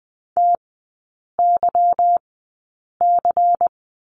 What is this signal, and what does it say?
Morse code, heard as a keyed tone, about 20 wpm, spaced as for 8 wpm, 690 Hz. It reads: TYC
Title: 198 Lesson 32e - Letter k - Three Characters 20wpm